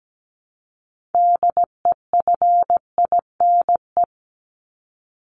Morse code, heard as a keyed tone, about 17 wpm, 700 Hz. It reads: DEFINE
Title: ED Suffix - Top 100 Words with Base Words 17wpm